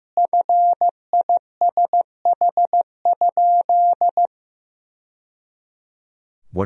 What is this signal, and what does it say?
Morse code, heard as a keyed tone, about 15 wpm, 700 Hz. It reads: FISH?
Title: Sentences from Top 1000 Words - Review Part 1 - 15wpm